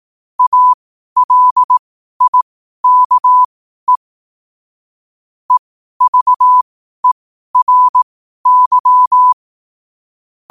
Morse code, heard as a keyed tone, about 18 wpm, spaced as for 12 wpm, 1000 Hz. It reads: ALIKE EVERY
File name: qotdfiles / 12wpm